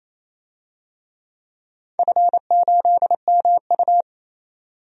Morse code, heard as a keyed tone, about 28 wpm, 700 Hz. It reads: F8MU
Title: DXCC-100 Call Signs - 4 Characters - 28wpm